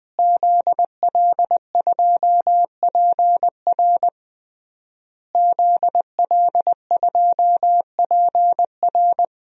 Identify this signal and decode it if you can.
Morse code, heard as a keyed tone, about 21 wpm, 700 Hz. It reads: ZL2PR ZL2PR
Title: DXCC-100 Call Signs - 5 Characters - Repeated 3x - 20wpm